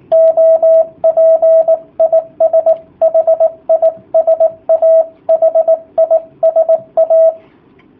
Morse code, heard as a keyed tone, about 18 wpm, 650 Hz, with radio noise. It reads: OP IS HISA HISA
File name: jh8cbh / 2019cw / sp18- w